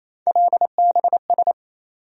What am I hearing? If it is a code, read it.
Morse code, heard as a keyed tone, about 28 wpm, 700 Hz. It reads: LBS
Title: USA Call Sign Postfixes - 3 Letters 28wpm